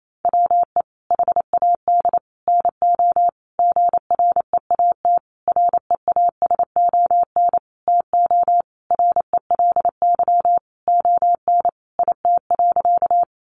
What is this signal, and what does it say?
Morse code, heard as a keyed tone, about 28 wpm, 700 Hz. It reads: WE HAD NO GREAT REASON TO RELY ON IT.
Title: Sentences from Top 1200 Words 28wpm